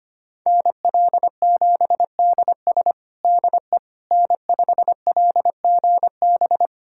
Morse code, heard as a keyed tone, about 25 wpm, 700 Hz. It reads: NL7DS DE N5LGB
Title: USA Callsign DE Callsign - 5 Characters - 25wpm